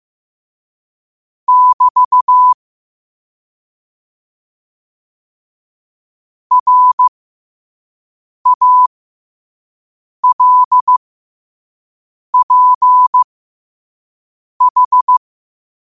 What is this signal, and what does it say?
Morse code, heard as a keyed tone, about 15 wpm, spaced as for 5 wpm, 1000 Hz. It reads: <BT> RALPH
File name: qotdfiles / 5wpm